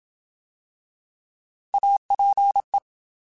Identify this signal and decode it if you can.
Morse code, heard as a keyed tone, about 27 wpm, 800 Hz. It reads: APE
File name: Alphabetical / Alphabetical mp3s / As @25